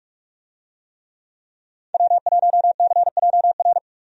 Morse code, heard as a keyed tone, about 46 wpm, 700 Hz. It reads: W1CJR
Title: USA Call Signs - 5 Characters - 45wpm